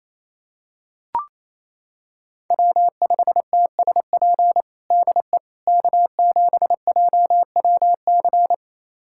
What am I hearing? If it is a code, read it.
Morse code, heard as a keyed tone, about 28 wpm, 690 Hz. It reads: W5TSP DE K7JWC
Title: USA Callsign DE Callsign - 5 Characters - 28wpm